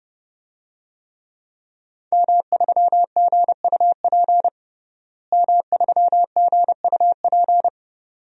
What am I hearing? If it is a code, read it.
Morse code, heard as a keyed tone, about 30 wpm, 700 Hz. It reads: M3GUP M3GUP
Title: DXCC-100 Call Signs - 5 Characters - Repeated 2x - 30wpm